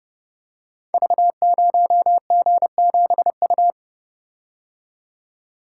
Morse code, heard as a keyed tone, about 28 wpm, 700 Hz. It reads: V0G7U
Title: Code Groups - 5 Letter:Numbers - 30wpm